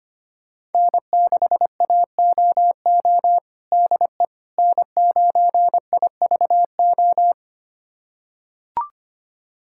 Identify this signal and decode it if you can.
Morse code, heard as a keyed tone, about 25 wpm, 700 Hz. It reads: N6AOO DE N9IVO
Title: USA Callsign DE Callsign - 5 Characters - 25wpm